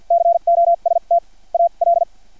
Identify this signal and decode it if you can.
Morse code, heard as a keyed tone, about 50 wpm, spaced as for 40 wpm, 680 Hz. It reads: COST AL
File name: archived / CRSnet